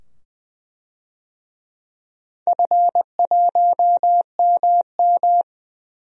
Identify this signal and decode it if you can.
Morse code, heard as a keyed tone, about 20 wpm, 700 Hz. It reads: F1MM
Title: DXCC-100 Call Signs - 4 Characters - 20wpm